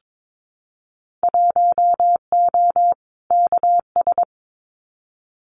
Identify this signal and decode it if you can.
Morse code, heard as a keyed tone, about 22 wpm, 700 Hz. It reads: 1O KS
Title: Winter Field Day Exchanges 22wpm